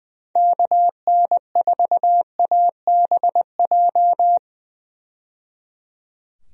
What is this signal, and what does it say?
Morse code, heard as a keyed tone, about 20 wpm, 700 Hz. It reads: KN4ABJ
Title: USA Call Signs - 6 Characters - 20wpm